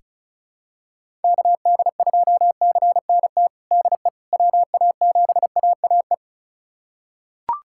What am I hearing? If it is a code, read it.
Morse code, heard as a keyed tone, about 36 wpm, 700 Hz. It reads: KD2CNT DE WA7AAE
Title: USA Callsign DE Callsign - 6 Characters - 35wpm